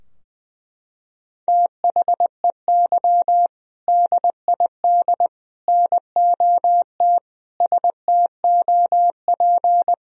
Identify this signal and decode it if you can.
Morse code, heard as a keyed tone, about 20 wpm, 700 Hz. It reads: THEY DID NOT STOP
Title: Sentences from Top 1000 Words - Review Part 2 - 20wpm